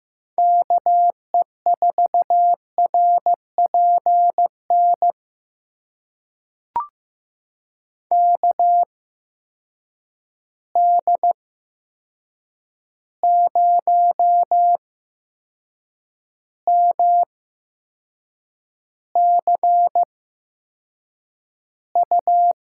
Morse code, heard as a keyed tone, about 15 wpm, spaced as for 6 wpm, 700 Hz. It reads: KE4RPN KD0MCU
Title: USA Call Signs - 6 Characters - 8x Character Spacing - 15wpm